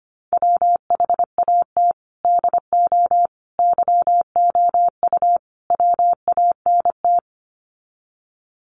Morse code, heard as a keyed tone, about 25 wpm, 700 Hz. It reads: WHAT DO YOU WANT
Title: Sets of 4 Words 25wpm